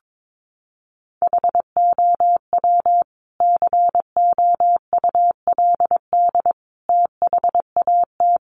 Morse code, heard as a keyed tone, about 22 wpm, 700 Hz. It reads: HOW COULD THAT
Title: Sentences from Top 400 Words 22wpm